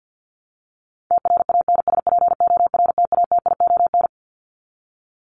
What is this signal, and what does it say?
Morse code, heard as a keyed tone, about 50 wpm, 700 Hz. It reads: TRANSPORTATION
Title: Top 1501-1750 Words 50wpm